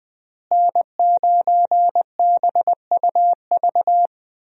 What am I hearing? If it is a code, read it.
Morse code, heard as a keyed tone, about 20 wpm, 700 Hz. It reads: N9BUV
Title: USA Call Signs - 5 Characters - Repeated 3x - 20wpm